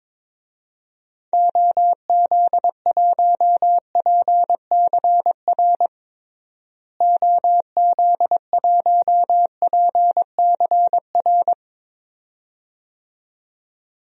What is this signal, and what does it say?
Morse code, heard as a keyed tone, about 22 wpm, 700 Hz. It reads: OZ1PCR OZ1PCR
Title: DXCC-100 Call Signs - 6 Characters - Repeated 2x - 22wpm